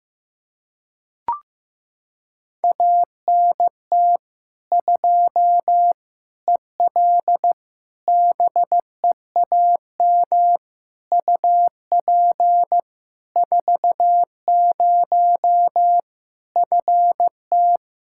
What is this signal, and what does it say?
Morse code, heard as a keyed tone, about 15 wpm, 700 Hz. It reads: ANT 2 EL BEAM UP 40 FT
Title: Practice QSO - Exchange 2 - Info - 15wpm